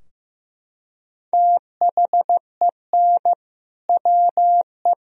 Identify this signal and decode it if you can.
Morse code, heard as a keyed tone, about 15 wpm, 700 Hz. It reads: THEN WE
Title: Sentences from Top 1000 Words - Review Part 2 - 15wpm